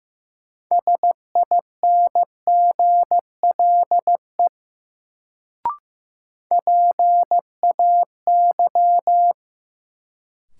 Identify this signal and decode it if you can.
Morse code, heard as a keyed tone, about 15 wpm, 700 Hz. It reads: SINGLE PAY
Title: Sentences from Top 700 Words - New Word Drill - 15wpm